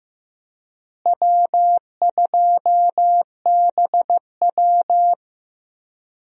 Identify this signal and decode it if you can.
Morse code, heard as a keyed tone, about 15 wpm, 700 Hz. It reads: W2BW
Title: USA Call Signs - 4 Characters - Repeated 3x - 15wpm